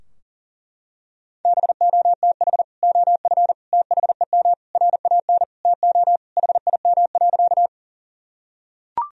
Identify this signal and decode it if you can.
Morse code, heard as a keyed tone, about 40 wpm, 700 Hz. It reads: BOTH OF THEM RAN TO HIM.
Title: Sentences from Top 1000 Words - Review Part 3 - 40wpm